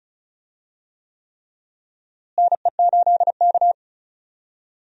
Morse code, heard as a keyed tone, about 34 wpm, 700 Hz. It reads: NE8K
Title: USA Call Signs - 4 Characters - Repeated 3x - 35wpm